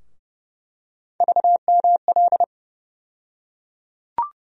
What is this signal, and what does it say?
Morse code, heard as a keyed tone, about 30 wpm, 700 Hz. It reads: VML
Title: USA Call Sign Postfixes - 3 Letters 30wpm